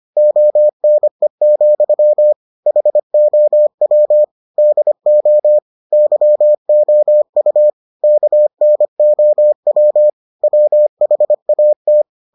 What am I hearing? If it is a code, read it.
Morse code, heard as a keyed tone, about 25 wpm, 600 Hz. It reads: ONE, HOW DO YOU KNOW WHAT